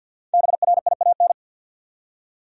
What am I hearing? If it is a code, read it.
Morse code, heard as a keyed tone, about 52 wpm, 700 Hz. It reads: BRIAN
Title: USA Common Names - Max 5 Letters - 50wpm